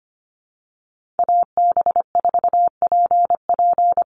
Morse code, heard as a keyed tone, about 25 wpm, 700 Hz. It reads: AB4PP
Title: NA SOTA Chaser Callsigns 25wpm